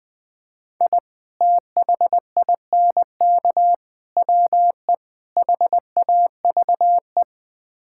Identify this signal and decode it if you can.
Morse code, heard as a keyed tone, about 20 wpm, 700 Hz. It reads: I THINK WE HAVE
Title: Build A Phrase - 4-Word Phrases - 20wpm